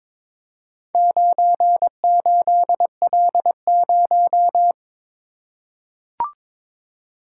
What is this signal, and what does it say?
Morse code, heard as a keyed tone, about 22 wpm, 700 Hz. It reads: 98L0
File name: practice / 22wpm / Four Letter-Number Combinations - 22wpm